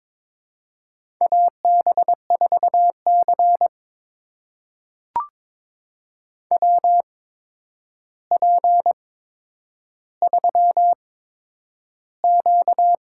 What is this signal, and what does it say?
Morse code, heard as a keyed tone, about 22 wpm, spaced as for 9 wpm, 700 Hz. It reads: AB4C WP3Q
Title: USA Call Signs - 4 Characters - 8x Character Spacing - 22wpm